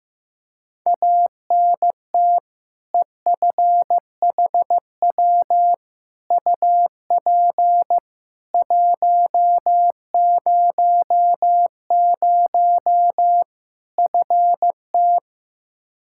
Morse code, heard as a keyed tone, about 16 wpm, 700 Hz. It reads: ANT EFHW UP 100 FT
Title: Practice QSO - Exchange 2 - Info - 15wpm